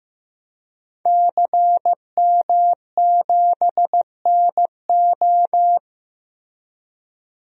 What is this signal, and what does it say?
Morse code, heard as a keyed tone, about 15 wpm, 700 Hz. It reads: CM7NO
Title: DXCC-100 Call Signs - 5 Characters - Repeated 3x - 15wpm